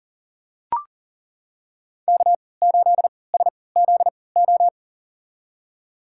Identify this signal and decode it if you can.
Morse code, heard as a keyed tone, about 40 wpm, spaced as for 23 wpm, 700 Hz. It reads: K8SZO
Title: USA Call Signs - 5 Characters - 3x Character Spacing - 40wpm